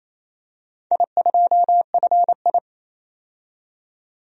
Morse code, heard as a keyed tone, about 28 wpm, 700 Hz. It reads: I2FI
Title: DXCC-100 Call Signs - 4 Characters - 28wpm